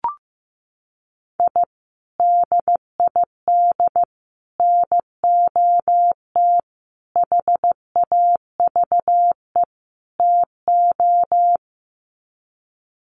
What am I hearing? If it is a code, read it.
Morse code, heard as a keyed tone, about 15 wpm, 700 Hz. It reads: I DID NOT HAVE TO
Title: Sets of 5 Words 15wpm